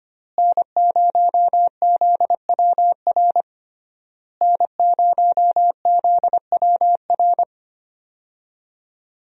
Morse code, heard as a keyed tone, about 25 wpm, 700 Hz. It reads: N0ZWR N0ZWR
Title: USA Call Signs - 5 Characters - Repeated 3x - 25wpm